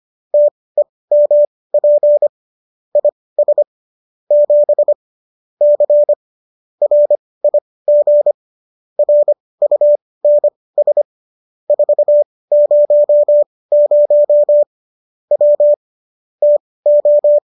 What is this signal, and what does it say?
Morse code, heard as a keyed tone, about 25 wpm, spaced as for 18 wpm, 590 Hz. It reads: TEMP IS 7 C RIG RUNS 400 W TO